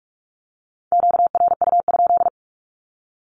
Morse code, heard as a keyed tone, about 46 wpm, 700 Hz. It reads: QRU?
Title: Q-Codes with Spoken Meaning - 45wpm